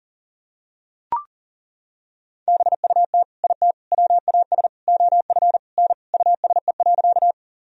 Morse code, heard as a keyed tone, about 40 wpm, 700 Hz. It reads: BUT IT WAS OF N USE.
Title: Sentences from Top 100 Words and Abbreviations 40wpm